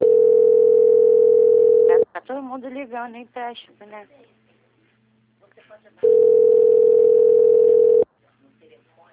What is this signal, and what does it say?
Ringback tone (North America), 440 + 480 Hz, with speech.
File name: out- - - - - .9619